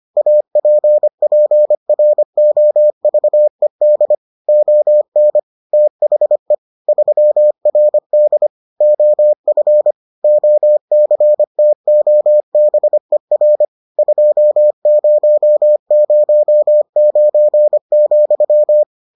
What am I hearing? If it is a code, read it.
Morse code, heard as a keyed tone, about 25 wpm, 600 Hz. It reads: APPROVED ON THE 3RD OF OCTOBER 2009,